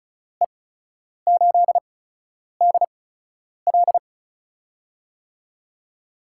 Morse code, heard as a keyed tone, about 36 wpm, spaced as for 10 wpm, 700 Hz. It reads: E8DL
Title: USA Call Signs - 8x Character Spacing - 35wpm